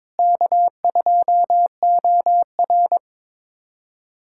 Morse code, heard as a keyed tone, about 22 wpm, 700 Hz. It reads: K2OR
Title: USA Callsign DE Callsign 22wpm